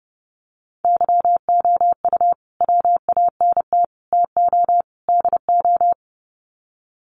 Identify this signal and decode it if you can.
Morse code, heard as a keyed tone, about 30 wpm, 700 Hz. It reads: YOU WANT TO DO
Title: Sets of 4 Words 30wpm